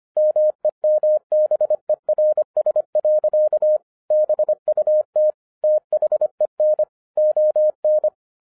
Morse code, heard as a keyed tone, about 25 wpm, 610 Hz. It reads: MEMBERS. BUT THEN ON
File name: 2012-01-mp3 / n06-FIRST FISTS CONTACT-25wpm